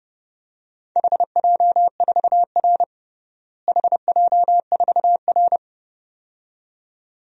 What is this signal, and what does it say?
Morse code, heard as a keyed tone, about 30 wpm, 700 Hz. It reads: HJ4R HJ4R
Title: DXCC-100 Call Signs - 4 Characters - Repeated 2x - 30wpm